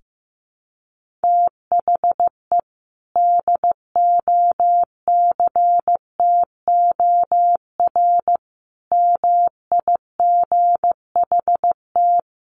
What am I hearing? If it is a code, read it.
Morse code, heard as a keyed tone, about 15 wpm, 700 Hz. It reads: THE DOCTOR MIGHT
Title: Sentences from Top 1500 Words 15wpm